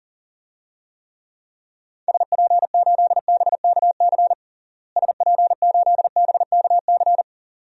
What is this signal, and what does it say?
Morse code, heard as a keyed tone, about 40 wpm, 700 Hz. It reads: SP8BKC SP8BKC
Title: DXCC-100 Call Signs - 6 Characters - Repeated 2x - 40wpm